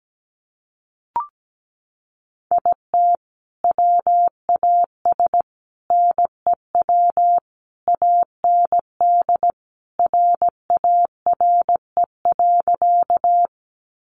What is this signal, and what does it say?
Morse code, heard as a keyed tone, about 17 wpm, 700 Hz. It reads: IT WAS NEW AND RARE.
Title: Sentences from Top 1100 Words 17wpm